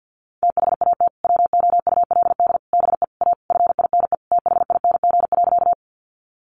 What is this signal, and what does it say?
Morse code, heard as a keyed tone, about 50 wpm, 700 Hz. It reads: THAT WOULD BE A FINE THING.
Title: Sentences from Top 500 Words 50wpm